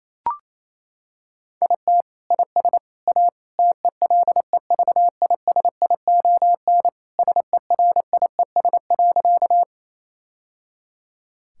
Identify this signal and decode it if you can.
Morse code, heard as a keyed tone, about 28 wpm, 700 Hz. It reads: IT IS A TELEVISION SERIES.
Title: Sentences from Top 1000 Words - Review Part 5 - 28wpm